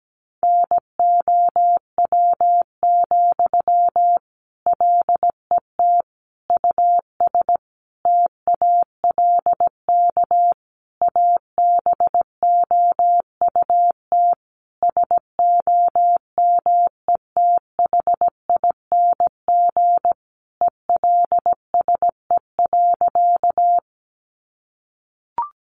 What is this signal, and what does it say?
Morse code, heard as a keyed tone, about 17 wpm, 700 Hz. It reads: NOW, LET US TALK ABOUT SOMETHING ELSE.
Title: Sentences from Top 800 Words 17wpm